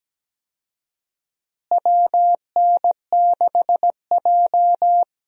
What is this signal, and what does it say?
Morse code, heard as a keyed tone, about 17 wpm, 700 Hz. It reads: WN6J
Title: USA Call Signs - 4 Characters - Repeated 2x - 17wpm